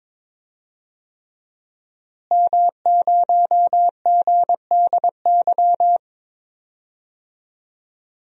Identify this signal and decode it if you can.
Morse code, heard as a keyed tone, about 22 wpm, 700 Hz. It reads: M0GDY
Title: DXCC-100 Call Signs - 5 Characters - Repeated 3x - 22wpm